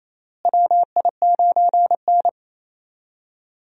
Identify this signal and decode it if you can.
Morse code, heard as a keyed tone, about 28 wpm, 700 Hz. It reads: WI9N
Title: USA Call Signs - 4 Characters - Repeated 3x - 28wpm